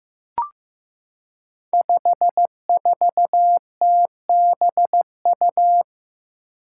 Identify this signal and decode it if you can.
Morse code, heard as a keyed tone, about 15 wpm, 700 Hz. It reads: 54TBU
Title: Five Challenging Letter-Number Combinations 15wpm